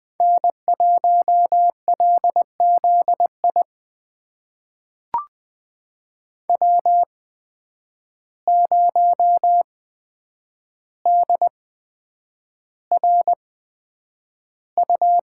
Morse code, heard as a keyed tone, about 20 wpm, spaced as for 9 wpm, 700 Hz. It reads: N1LZI W0DRU
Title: USA Call Signs - 8x Character Spacing - 20wpm